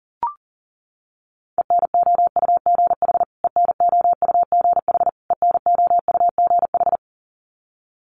Morse code, heard as a keyed tone, about 40 wpm, 700 Hz. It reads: ENOUGH ENOUGH ENOUGH
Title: Top 500 Words Repeated 40wpm